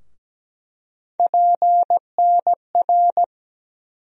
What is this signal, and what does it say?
Morse code, heard as a keyed tone, about 17 wpm, 700 Hz. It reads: PNR
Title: USA Call Sign Postfixes - 3 Letters 17wpm